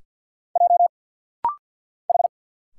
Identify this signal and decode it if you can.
Morse code, heard as a keyed tone, about 46 wpm, 690 Hz. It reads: J H
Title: Single Letters - Rapid-Fire 50wpm